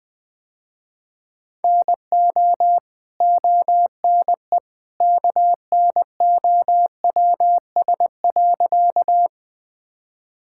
Morse code, heard as a keyed tone, about 21 wpm, 700 Hz. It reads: NO ONE KNOWS.
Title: Sentences with 2-Word Phrases - Spoken Beginning - 20wpm